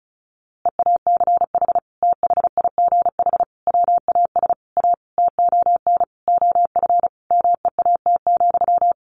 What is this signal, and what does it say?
Morse code, heard as a keyed tone, about 34 wpm, 700 Hz. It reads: EACH THIGH WAS A TON OF MEAT,
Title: A Sound Of Thunder 35wpm